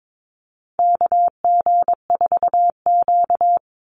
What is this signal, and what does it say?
Morse code, heard as a keyed tone, about 22 wpm, 700 Hz. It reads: KG4Q
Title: Call Signs - 22wpm